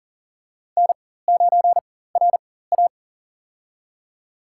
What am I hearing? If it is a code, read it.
Morse code, heard as a keyed tone, about 40 wpm, spaced as for 19 wpm, 700 Hz. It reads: N9RA
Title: USA Call Signs - 4x Character Spacing - 40wpm